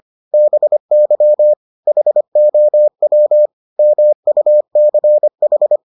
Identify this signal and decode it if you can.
Morse code, heard as a keyed tone, about 25 wpm, 600 Hz. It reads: BY HOW MUCH